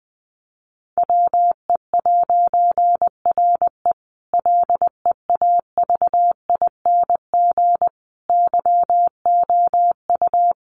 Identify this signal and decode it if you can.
Morse code, heard as a keyed tone, about 20 wpm, 700 Hz. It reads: WE'RE LEAVING YOU
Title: A Sound Of Thunder 20wpm